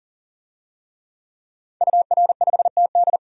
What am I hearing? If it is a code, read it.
Morse code, heard as a keyed tone, about 40 wpm, 700 Hz. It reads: UR5TD
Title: DXCC-100 Call Signs - 5 Characters - 40wpm